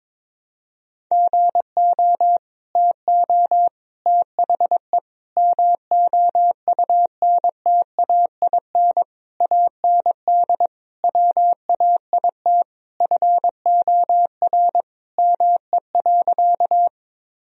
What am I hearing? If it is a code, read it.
Morse code, heard as a keyed tone, about 22 wpm, 700 Hz. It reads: GO TO THE MOUNTAIN AND WAIT FOR ME.
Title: Sentences from Top 1000 Words - Review Part 1 - 22wpm